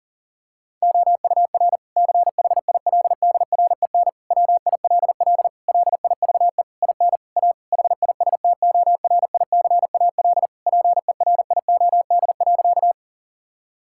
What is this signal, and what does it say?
Morse code, heard as a keyed tone, about 37 wpm, 700 Hz. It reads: OUR CHILDREN WILL LIVE IN A HISTORICAL PERIOD.
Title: Sentences from Top 1000 Words - Review Part 5 - 40wpm